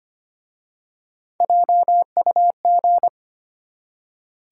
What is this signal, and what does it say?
Morse code, heard as a keyed tone, about 25 wpm, 700 Hz. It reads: JUG
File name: practice / 25wpm / Easy Words with Hints - 25wpm